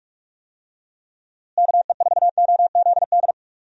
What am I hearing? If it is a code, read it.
Morse code, heard as a keyed tone, about 43 wpm, 700 Hz. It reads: KE4OZD
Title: USA Call Signs - 6 Characters - Repeated 3x - 45wpm